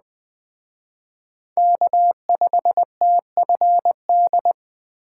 Morse code, heard as a keyed tone, about 25 wpm, 700 Hz. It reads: K5TFD
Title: Speed Racing - USA 1x3 Call Signs 20wpm